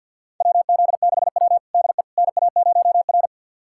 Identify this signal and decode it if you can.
Morse code, heard as a keyed tone, about 50 wpm, 700 Hz. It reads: WZ6W DE NA0R
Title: USA Callsign DE Callsign - 4 Characters - 50wpm